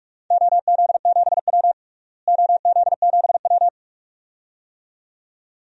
Morse code, heard as a keyed tone, about 47 wpm, 700 Hz. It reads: OZ7W OZ7W
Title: DXCC-100 Call Signs - 4 Characters - Repeated 2x - 45wpm